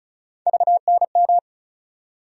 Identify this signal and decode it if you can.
Morse code, heard as a keyed tone, about 34 wpm, 700 Hz. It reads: VNM
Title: USA Call Sign Postfixes - 3 Letters 35wpm